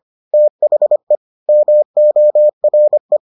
Morse code, heard as a keyed tone, about 25 wpm, 600 Hz. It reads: THE MORE